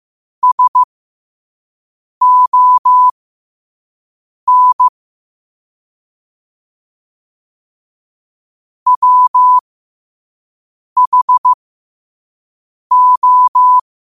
Morse code, heard as a keyed tone, about 15 wpm, spaced as for 5 wpm, 1000 Hz. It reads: SON WHO
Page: learn-to-code